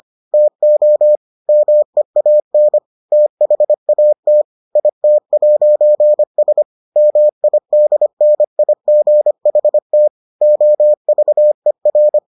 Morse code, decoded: TO MEAN THAT IT'S MIDNIGHT OVER

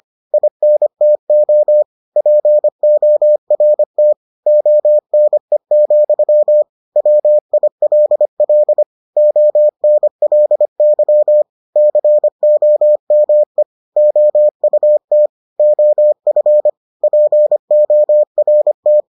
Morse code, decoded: INTO PORT ONE, WILL ONLY COME OUT OF PORT

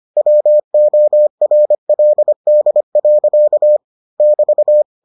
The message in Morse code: WORLD. <BT>

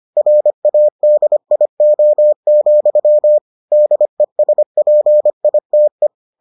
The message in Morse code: RADIO, DESPITE

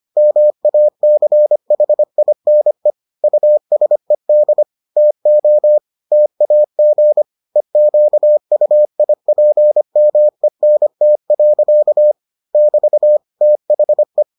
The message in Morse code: MACHINE USED TO TAG EQUIPMENT. <BT> THE

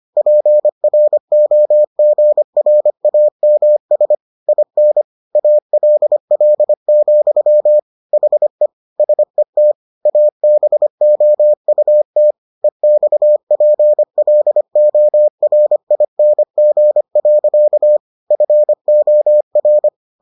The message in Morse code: PROGRAMS IN ALL, HE SET ABOUT EXPLORING. FOR

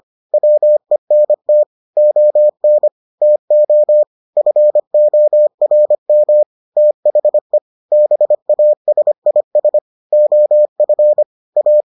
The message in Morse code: WENT ON TO FORM THE BASIS OF A

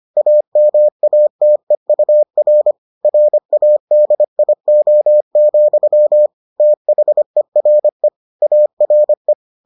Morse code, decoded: AMATEUR RADIO, THERE ARE

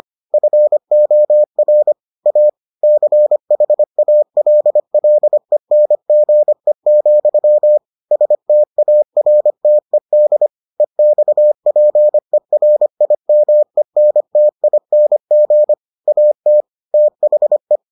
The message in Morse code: FOR A CHALLENGE, STARTED EXPERIMENTING AT THE

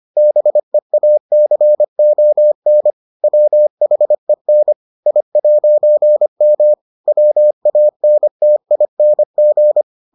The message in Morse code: BEACON WHEN I'M WANTING

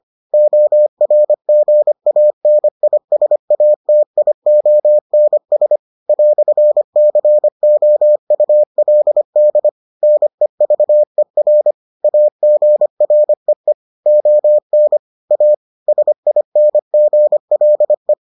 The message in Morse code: ORGANISATIONS "COULD NEVER AGREE ON A SINGLE

